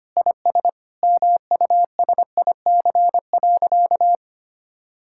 Morse code, heard as a keyed tone, about 25 wpm, 700 Hz. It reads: IS MUSIC.
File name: practice / 25wpm / Sentences with 3-Word Phrases - 25wpm